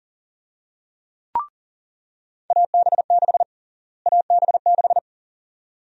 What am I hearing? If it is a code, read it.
Morse code, heard as a keyed tone, about 40 wpm, 700 Hz. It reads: AB6 AB6
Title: USA Call Sign Prefixes - Repeated 2x - 40wpm